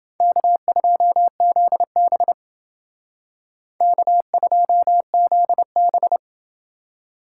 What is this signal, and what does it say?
Morse code, heard as a keyed tone, about 30 wpm, 700 Hz. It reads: K2ZB K2ZB